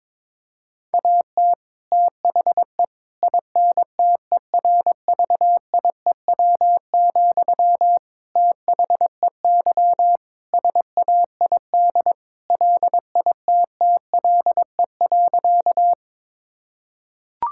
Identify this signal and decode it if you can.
Morse code, heard as a keyed tone, about 22 wpm, 700 Hz. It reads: AT THE INTERVIEW, THEY SAID LITTLE.
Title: Sentences from Top 1600 Words 22wpm